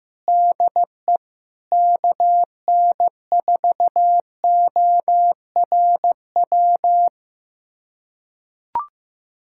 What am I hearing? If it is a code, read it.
Morse code, heard as a keyed tone, about 15 wpm, 700 Hz. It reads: DE KN4ORW
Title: USA Callsign DE Callsign - 6 Characters - 15wpm